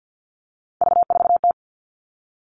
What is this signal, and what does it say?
Morse code, heard as a keyed tone, about 50 wpm, 700 Hz. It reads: V4T
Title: Three Challenging Letter-Number Combinations 50wpm